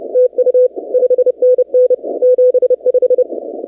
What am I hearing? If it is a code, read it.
Morse code, heard as a keyed tone, about 28 wpm, 500 Hz. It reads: TU 5NN 75
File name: 2019-cw-wpx